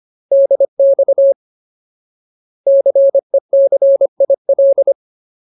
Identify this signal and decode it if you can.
Morse code, heard as a keyed tone, about 25 wpm, 550 Hz. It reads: DX CECIL
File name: Ultimate Word List / 25WPM Ultimate